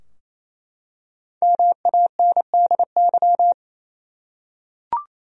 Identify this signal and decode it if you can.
Morse code, heard as a keyed tone, about 28 wpm, 700 Hz. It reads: MANDY
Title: USA Common Names - 28wpm